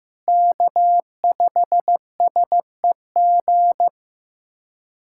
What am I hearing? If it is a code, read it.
Morse code, heard as a keyed tone, about 15 wpm, 700 Hz. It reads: K5SEG
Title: USA Callsign DE Callsign - 5 Characters - 15wpm